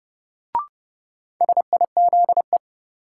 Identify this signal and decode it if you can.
Morse code, heard as a keyed tone, about 30 wpm, 700 Hz. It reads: SIZE
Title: Sets of 3 Words - New Word Drill - 30wpm